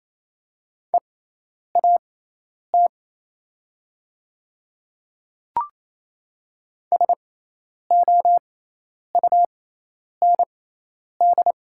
Morse code, heard as a keyed tone, about 27 wpm, spaced as for 10 wpm, 700 Hz. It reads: EAT SOUND
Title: 6x Character Spacing - Top 300 Words - 28wpm